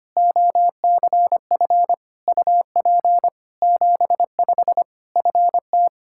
Morse code, heard as a keyed tone, about 25 wpm, 700 Hz. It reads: OCF UP 75 FT
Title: K1USN SST QSO Practice 25wpm